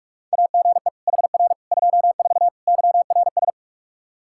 Morse code, heard as a keyed tone, about 46 wpm, 700 Hz. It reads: AGE HR 24 YRS
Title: K1USN SST QSO Practice 45wpm